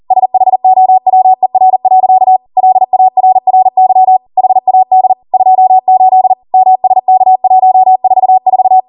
Morse code, heard as a keyed tone, about 40 wpm, 750 Hz. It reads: SHOWER. LARRY HAD 28 MSK144